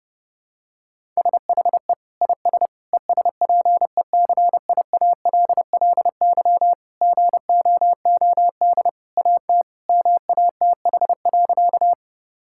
Morse code, heard as a keyed tone, about 30 wpm, 700 Hz. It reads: SHE IS ESPECIALLY GOOD AT MATH.
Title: Sentences from Top 1000 Words - Review Part 6 - 30wpm